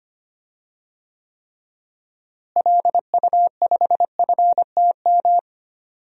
Morse code, heard as a keyed tone, about 25 wpm, 700 Hz. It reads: LU5FTM
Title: DXCC-100 Call Signs - 6 Characters - Repeated 2x - 25wpm